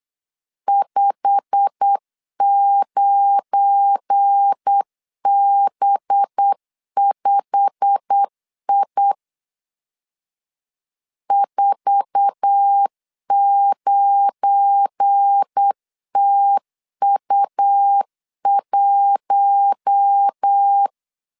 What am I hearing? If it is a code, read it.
Morse code, heard as a keyed tone, about 8 wpm, 800 Hz. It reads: 59B5I 49TU1
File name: LEZIONI / 8 YJ45 / 8 YJ45 8wpm 3214